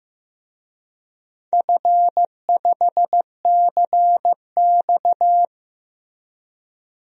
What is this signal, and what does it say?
Morse code, heard as a keyed tone, about 15 wpm, 700 Hz. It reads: F5CX
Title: DXCC-100 Call Signs - 4 Characters - Repeated 2x - 15wpm